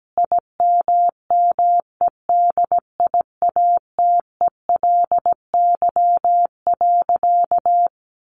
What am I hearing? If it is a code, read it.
Morse code, heard as a keyed tone, about 17 wpm, 700 Hz. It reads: IMMEDIATELY.
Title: Sentences from Top 1100 Words 17wpm